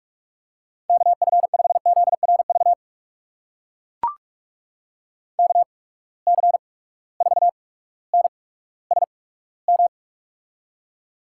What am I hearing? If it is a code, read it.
Morse code, heard as a keyed tone, about 44 wpm, spaced as for 19 wpm, 700 Hz. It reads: KF5ZRV KC4NSM
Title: USA Call Signs - 6 Characters - 8x Character Spacing - 45wpm